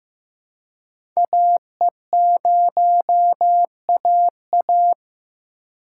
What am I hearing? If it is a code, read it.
Morse code, heard as a keyed tone, about 15 wpm, 700 Hz. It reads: AE0AA
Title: USA Call Signs - 5 Characters - 3x Character Spacing - 15wpm